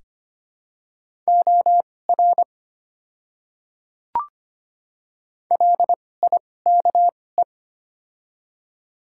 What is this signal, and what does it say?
Morse code, heard as a keyed tone, about 25 wpm, spaced as for 18 wpm, 700 Hz. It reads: OR LIKE
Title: 2x Character Spacing - Top 200 Words - 25wpm